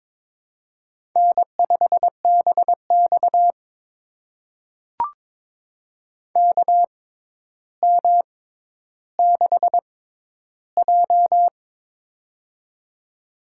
Morse code, heard as a keyed tone, about 22 wpm, spaced as for 11 wpm, 700 Hz. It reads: N5BX KM6J
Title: USA Call Signs - 4 Characters - 6x Character Spacing - 22wpm